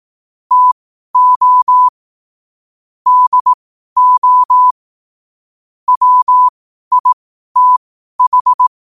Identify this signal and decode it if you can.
Morse code, heard as a keyed tone, about 18 wpm, spaced as for 12 wpm, 1000 Hz. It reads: TO DO WITH